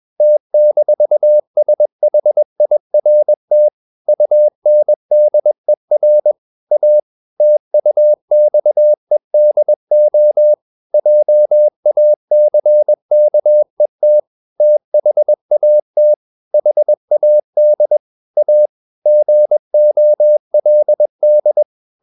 Morse code, heard as a keyed tone, about 21 wpm, 600 Hz. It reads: T-SHIRT UNDER A TUXEDO JACKET THAT HAD A GOLD
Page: CW med Gnister